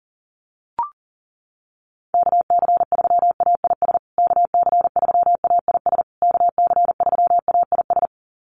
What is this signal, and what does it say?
Morse code, heard as a keyed tone, about 40 wpm, 700 Hz. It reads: KC3AIS KC3AIS KC3AIS
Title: USA Call Signs Repeated 3x 40wpm